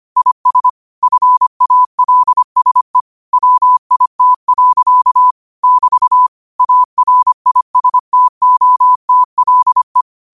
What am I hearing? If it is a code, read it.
Morse code, heard as a keyed tone, about 25 wpm, 1000 Hz. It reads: IS FALSE WIT. <BT> ARISTOTLE
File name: qotdfiles / 25wpm